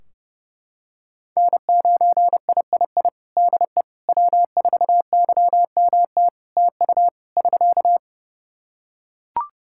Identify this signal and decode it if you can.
Morse code, heard as a keyed tone, about 32 wpm, 700 Hz. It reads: N9III DE W4YMT TU <SK>
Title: Practice QSO - Exchange 4 - Ending - 30wpm